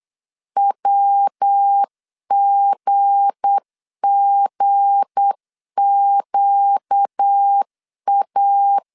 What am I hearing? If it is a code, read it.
Morse code, heard as a keyed tone, about 8 wpm, 800 Hz. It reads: WGGQA
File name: LEZIONI / 7 GF9Z / 7 G 8wpm 3118